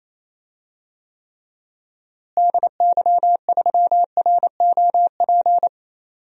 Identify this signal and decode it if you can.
Morse code, heard as a keyed tone, about 28 wpm, 700 Hz. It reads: DY3ROP
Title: DXCC-100 Call Signs - 6 Characters - Repeated 2x - 28wpm